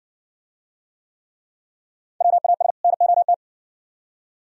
Morse code, heard as a keyed tone, about 57 wpm, 700 Hz. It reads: WAS NOT